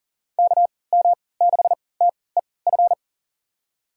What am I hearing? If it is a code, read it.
Morse code, heard as a keyed tone, about 40 wpm, spaced as for 23 wpm, 700 Hz. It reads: KM6TEF
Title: USA Call Signs - 3x Character Spacing - 40wpm